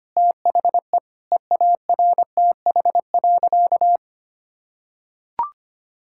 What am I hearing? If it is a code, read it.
Morse code, heard as a keyed tone, about 25 wpm, 700 Hz. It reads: THE EARTH.
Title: Sentences from Top 1600 Words 25wpm